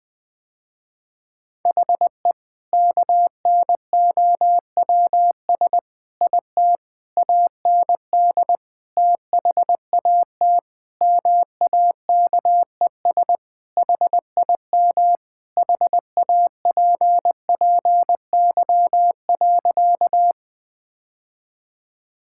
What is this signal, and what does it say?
Morse code, heard as a keyed tone, about 19 wpm, 700 Hz. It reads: HE KNOWS IT AND THAT MAKES HIM HAPPY.
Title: Sentences with 2-Word Phrases - Spoken Beginning - 20wpm